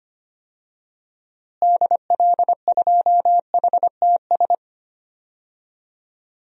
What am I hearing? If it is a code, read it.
Morse code, heard as a keyed tone, about 25 wpm, 700 Hz. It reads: DL2HTS
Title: DXCC-100 Call Signs - 6 Characters - Repeated 3x - 25wpm